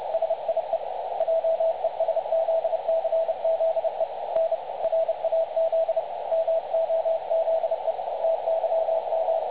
Morse code, heard as a keyed tone, about 31 wpm, 660 Hz, with radio noise. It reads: SIE JESZCZE N RAZ MOZE TYM
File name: dlazaawansowanych / nagraniazpasma / websdr recording 3565.0kHz